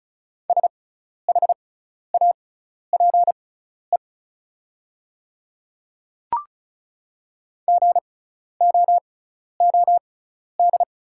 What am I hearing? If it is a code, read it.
Morse code, heard as a keyed tone, about 34 wpm, spaced as for 12 wpm, 690 Hz. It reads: SHAPE GOOD
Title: 6x Character Spacing - Top 500 Words - 35wpm